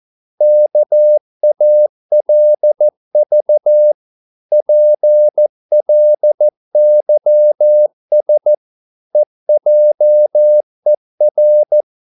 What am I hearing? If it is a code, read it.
Morse code, heard as a keyed tone, about 14 wpm, 600 Hz. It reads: KALV PLYS EJER
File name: Korte ord DK 14wpm